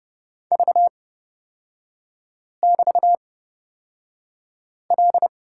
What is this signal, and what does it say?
Morse code, heard as a keyed tone, about 30 wpm, spaced as for 5 wpm, 700 Hz. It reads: V<BT>L
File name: Lesson 38e - Letter q - Three Characters 30wpm